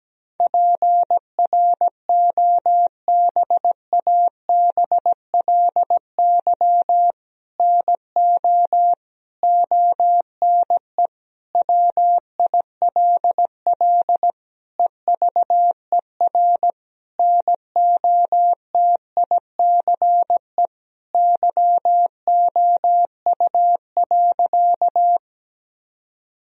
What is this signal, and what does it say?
Morse code, heard as a keyed tone, about 17 wpm, 700 Hz. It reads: PROBABLY NO ONE WILL EVER NOTICE YOU.
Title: Sentences from Top 1000 Words - Review Part 4 - 17wpm